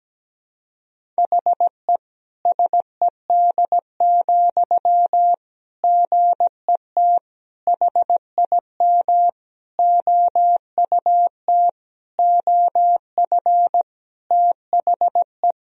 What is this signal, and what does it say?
Morse code, heard as a keyed tone, about 17 wpm, 700 Hz. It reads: HE SED, GET HIM OUT OF THE
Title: Sentences from Top 400 Words and Abbreviations 17wpm